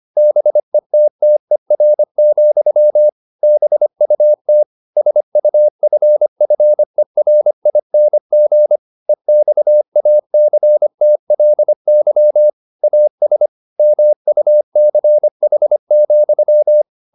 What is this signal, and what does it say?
Morse code, heard as a keyed tone, about 25 wpm, 600 Hz. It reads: BETTER, BUT SUFFERING EXACTLY AS MUCH,
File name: the old man and the sea part 1 25wpm